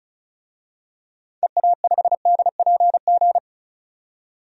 Morse code, heard as a keyed tone, about 36 wpm, 700 Hz. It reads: EA5DPG
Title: DXCC-100 Call Signs - 6 Characters - 35wpm